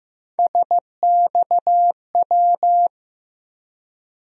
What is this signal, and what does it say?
Morse code, heard as a keyed tone, about 15 wpm, 700 Hz. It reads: SXW
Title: USA Call Sign Postfixes - 3 Letters 15wpm